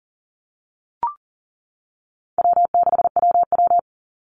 Morse code, heard as a keyed tone, about 40 wpm, 700 Hz. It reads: W6WW
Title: NA SOTA Chaser Callsigns 40wpm